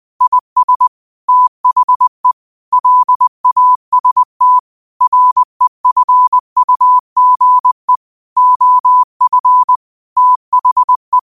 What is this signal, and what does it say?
Morse code, heard as a keyed tone, about 20 wpm, 1000 Hz. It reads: IS THE LAST REFUGE OF THE
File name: qotdfiles / 20wpm